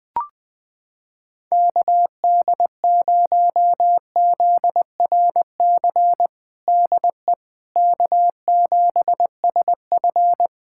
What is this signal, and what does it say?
Morse code, heard as a keyed tone, about 20 wpm, 700 Hz. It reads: KD0ZRC DE K7SF
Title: Practice QSO - Exchange 4 - Ending - 20wpm